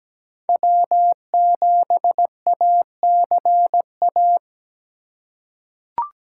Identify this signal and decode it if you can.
Morse code, heard as a keyed tone, about 17 wpm, 700 Hz. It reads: W7ACA
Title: USA Call Signs - 4x Character Spacing - 17wpm